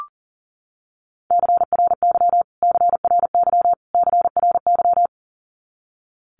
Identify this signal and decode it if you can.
Morse code, heard as a keyed tone, about 40 wpm, 700 Hz. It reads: CRY CRY CRY
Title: Top 500 Words Repeated 40wpm